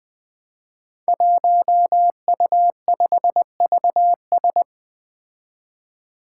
Morse code, heard as a keyed tone, about 20 wpm, 700 Hz. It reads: 1U5VS
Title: Five Challenging Letter-Number Combinations 20wpm